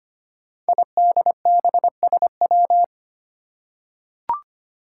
Morse code, heard as a keyed tone, about 25 wpm, 700 Hz. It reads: IDBSW
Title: Five Challenging Letter-Number Combinations 25wpm